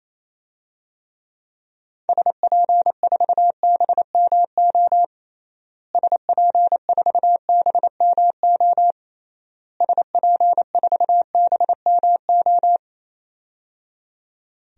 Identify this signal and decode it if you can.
Morse code, heard as a keyed tone, about 28 wpm, 700 Hz. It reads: SP4BMO SP4BMO SP4BMO
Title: DXCC-100 Call Signs - 6 Characters - Repeated 3x - 28wpm